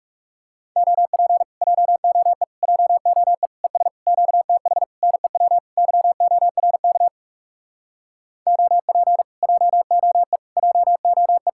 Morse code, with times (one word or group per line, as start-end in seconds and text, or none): 0.8-1.4 OP
1.6-2.4 JOE
2.6-3.5 JOE
3.6-3.9 ES
4.1-4.8 QTH
5.0-5.6 NEW
5.8-7.1 YORK
8.5-9.2 OP
9.4-10.4 JOE
10.6-11.5 JOE